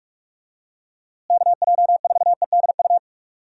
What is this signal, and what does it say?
Morse code, heard as a keyed tone, about 46 wpm, 700 Hz. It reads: KJ4EDU
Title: USA Call Signs - 6 Characters - Repeated 3x - 45wpm